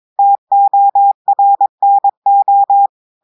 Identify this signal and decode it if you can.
Morse code, heard as a keyed tone, about 22 wpm, 800 Hz. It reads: TORNO
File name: audios / portugais / portugais texte en clair 1 vitesse 22 mots minutes